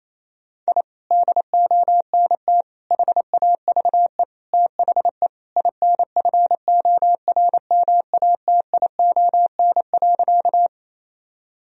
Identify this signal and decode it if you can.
Morse code, heard as a keyed tone, about 28 wpm, 700 Hz. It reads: I DONT HAVE THE INFORMATION.
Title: Sentences from Top 1000 Words - Review Part 3 - 28wpm